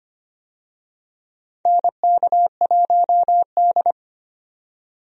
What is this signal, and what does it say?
Morse code, heard as a keyed tone, about 25 wpm, 700 Hz. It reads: NK1D
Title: USA Call Signs - 4 Characters - 25wpm